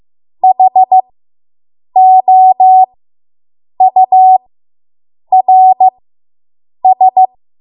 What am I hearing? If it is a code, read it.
Morse code, heard as a keyed tone, about 15 wpm, spaced as for 7 wpm, 750 Hz. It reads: HOURS